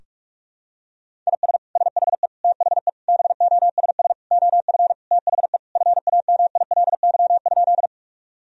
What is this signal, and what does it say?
Morse code, heard as a keyed tone, about 47 wpm, 700 Hz. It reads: IS SHE THE BOSS OF THE FAMILY?
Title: Sentences from Top 1000 Words - Review Part 4 - 45wpm